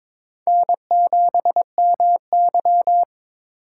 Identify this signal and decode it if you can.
Morse code, heard as a keyed tone, about 22 wpm, 700 Hz. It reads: N7MY
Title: USA Call Signs - 8x Character Spacing - 22wpm